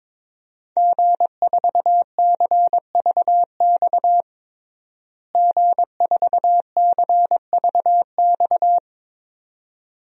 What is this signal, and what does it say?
Morse code, heard as a keyed tone, about 22 wpm, 700 Hz. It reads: G4CVX G4CVX
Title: DXCC-100 Call Signs - 5 Characters - Repeated 2x - 22wpm